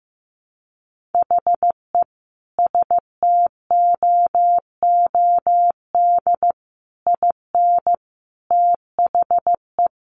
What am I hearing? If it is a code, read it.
Morse code, heard as a keyed tone, about 14 wpm, 700 Hz. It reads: HE STOOD IN THE
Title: Sentences from Top 1400 Words 15wpm